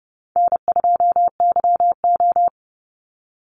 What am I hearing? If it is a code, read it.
Morse code, heard as a keyed tone, about 30 wpm, 700 Hz. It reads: N2YO
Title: Easy Call Signs Repeated 2x - 30wpm